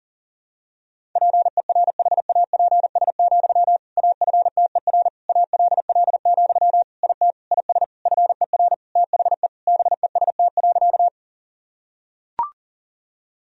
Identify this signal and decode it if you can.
Morse code, heard as a keyed tone, about 40 wpm, 700 Hz. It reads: PERHAPS, AFTER ALL, IT IS FER THE BEST.
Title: Sentences from Top 500 Words and Abbreviations 40wpm